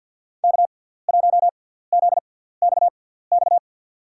Morse code, heard as a keyed tone, about 52 wpm, spaced as for 17 wpm, 700 Hz. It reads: K1ZXX
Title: USA Call Signs - 6x Character Spacing - 50wpm